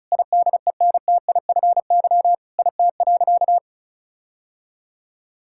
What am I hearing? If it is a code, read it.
Morse code, heard as a keyed tone, about 34 wpm, 700 Hz. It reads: IDENTIFY IT.
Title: Sentences from Top 1000 Words - Review Part 6 - 35wpm